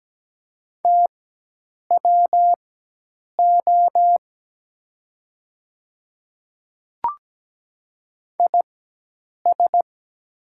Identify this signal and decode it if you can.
Morse code, heard as a keyed tone, about 17 wpm, spaced as for 8 wpm, 700 Hz. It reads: TWO IS
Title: 4x Character Spacing - Top 100 Words - 17wpm